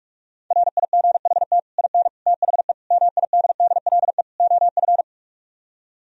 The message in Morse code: RIGHT IN THE MIDDLE OF

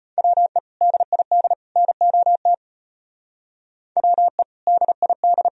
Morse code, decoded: WE DID NOT WE DID

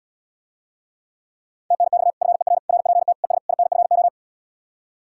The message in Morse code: TNX FER RPT ES INFO

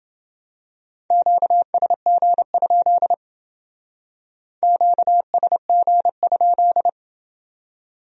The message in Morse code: QSG? QSG?

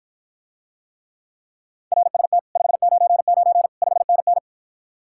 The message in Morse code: RST 599 5NN

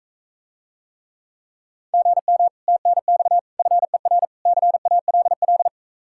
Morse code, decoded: GM TNX FER CALL

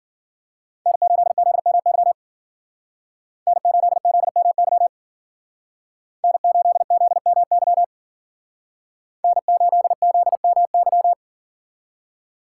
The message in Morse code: N8ZMY N8ZMY N8ZMY N8ZMY

